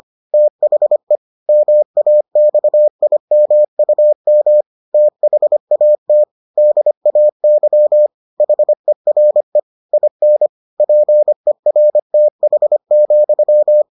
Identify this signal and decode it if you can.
Morse code, heard as a keyed tone, about 25 wpm, 600 Hz. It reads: THE MAXIMUM THAT DAY HERE IN PERTH,